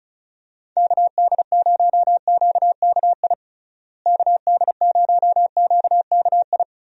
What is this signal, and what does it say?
Morse code, heard as a keyed tone, about 36 wpm, 700 Hz. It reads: KD0QKI KD0QKI
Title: USA Call Signs - 6 Characters - Repeated 2x - 35wpm